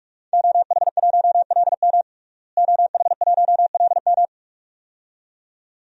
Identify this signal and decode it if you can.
Morse code, heard as a keyed tone, about 46 wpm, 700 Hz. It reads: OH1LM OH1LM
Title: DXCC-100 Call Signs - 5 Characters - Repeated 2x - 45wpm